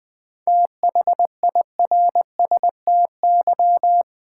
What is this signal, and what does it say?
Morse code, heard as a keyed tone, about 20 wpm, 700 Hz. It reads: THIRSTY
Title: 7 Letter Words - First 3 Letters Given - 20wpm